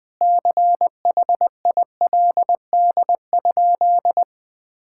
Morse code, heard as a keyed tone, about 20 wpm, 700 Hz. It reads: CHILD?
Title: Sentences from Top 1600 Words 20wpm